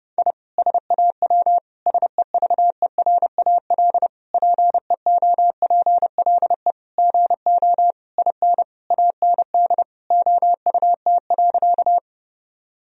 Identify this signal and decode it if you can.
Morse code, heard as a keyed tone, about 30 wpm, 700 Hz. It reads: I SAW SEVERAL PEOPLE GO IN AND OUT.
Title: Sentences from Top 1000 Words - Review Part 2 - 30wpm